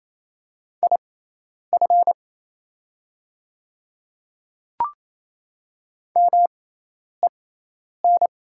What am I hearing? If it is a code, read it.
Morse code, heard as a keyed tone, about 28 wpm, spaced as for 10 wpm, 690 Hz. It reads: IF MEN
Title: 6x Character Spacing - Top 200 Words - 28wpm